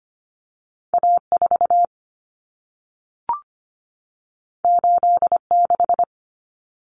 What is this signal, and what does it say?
Morse code, heard as a keyed tone, about 25 wpm, 700 Hz. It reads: A4 86
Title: Common Cut Numbers - 2-Digit 25wpm